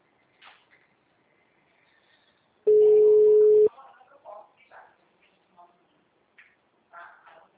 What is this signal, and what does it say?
Ringback tone (Europe, 425 Hz), with speech.